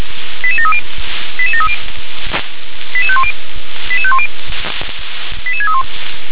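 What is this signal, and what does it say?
Five-tone selective call (ZVEI), heard as quick runs of five tones, tones of about 70 ms. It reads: 88730 88730 88510 88510 88631
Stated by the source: ZVEI five-tone selective call received by SDR.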